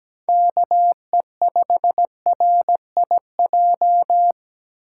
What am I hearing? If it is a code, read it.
Morse code, heard as a keyed tone, about 17 wpm, 700 Hz. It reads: KE5RIJ
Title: USA Call Signs - 6 Characters - 2x Character Spacing - 17wpm